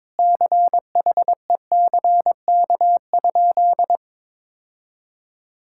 Morse code, heard as a keyed tone, about 22 wpm, 700 Hz. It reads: CHECK?
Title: Sentences from Top 1000 Words - Review Part 6 - 22wpm